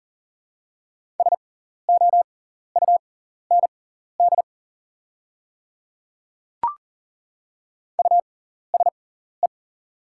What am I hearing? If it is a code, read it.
Morse code, heard as a keyed tone, about 42 wpm, spaced as for 14 wpm, 690 Hz. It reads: SOUND USE
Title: 6x Character Spacing - Top 100 Words - 40wpm